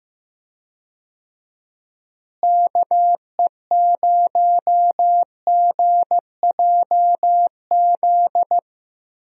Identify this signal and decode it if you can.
Morse code, heard as a keyed tone, about 15 wpm, 700 Hz. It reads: KE0GJZ
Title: USA Call Signs - 6 Characters - Repeated 2x - 15wpm